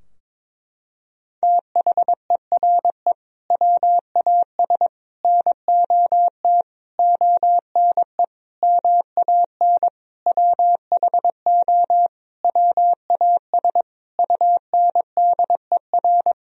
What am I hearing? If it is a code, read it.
Morse code, heard as a keyed tone, about 22 wpm, 700 Hz. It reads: THERE WAS NOT ONE MAN WHO WAS UNDER
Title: Sentences from Top 1000 Words - Review Part 2 - 22wpm